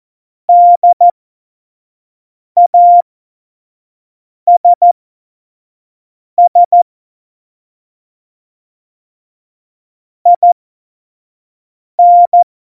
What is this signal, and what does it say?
Morse code, heard as a keyed tone, about 14 wpm, spaced as for 5 wpm, 700 Hz. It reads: DASS IN
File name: Morsekurs / Witze / Witz 19 Mai 2020, 25 F70